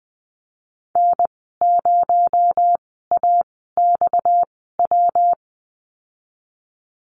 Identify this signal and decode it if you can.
Morse code, heard as a keyed tone, about 20 wpm, spaced as for 15 wpm, 700 Hz. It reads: N0AXW
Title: 231 Lesson 37g - Letter x - Call Signs 20wpm